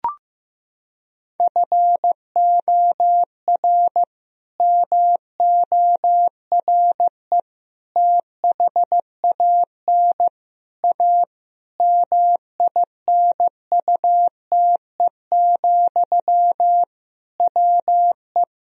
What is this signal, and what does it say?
Morse code, heard as a keyed tone, about 15 wpm, 700 Hz. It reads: FOR MORE THAN A MINUTE, WE
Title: Sentences from Top 1000 Words - Review Part 2 - 15wpm